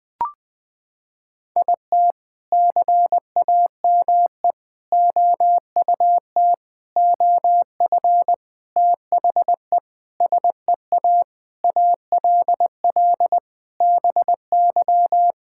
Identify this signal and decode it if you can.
Morse code, heard as a keyed tone, about 20 wpm, 700 Hz. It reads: IT CAME OUT OF THE SEA ALL BY
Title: Sentences from Top 1000 Words - Review Part 5 - 20wpm